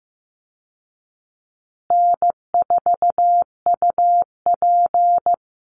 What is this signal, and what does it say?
Morse code, heard as a keyed tone, about 15 wpm, 700 Hz. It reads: N4UP
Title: Easy Call Signs Repeated 3x - 15wpm